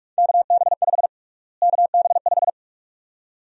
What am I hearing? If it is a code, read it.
Morse code, heard as a keyed tone, about 41 wpm, 700 Hz. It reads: KB5 KB5
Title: USA Call Sign Prefixes - Repeated 2x - 45wpm